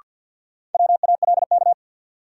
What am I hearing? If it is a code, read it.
Morse code, heard as a keyed tone, about 51 wpm, 700 Hz. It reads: WALK
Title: Sentences from Top 300 Words - New Word Drill - 50wpm